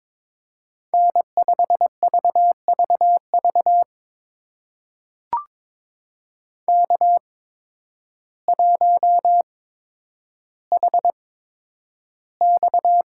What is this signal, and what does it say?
Morse code, heard as a keyed tone, about 22 wpm, spaced as for 10 wpm, 700 Hz. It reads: N5VVV K1HX
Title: USA Call Signs - 8x Character Spacing - 22wpm